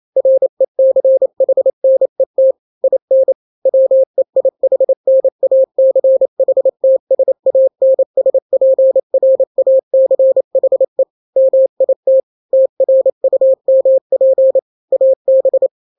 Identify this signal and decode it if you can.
Morse code, heard as a keyed tone, about 28 wpm, 530 Hz. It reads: RECHNET IN WEIHNACHTSANSPRACHE MIT TRUMP AB
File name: daily / 27wpm